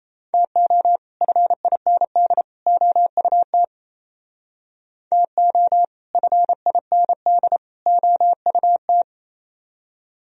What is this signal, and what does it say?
Morse code, heard as a keyed tone, about 37 wpm, 700 Hz. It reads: TO FIND OUT TO FIND OUT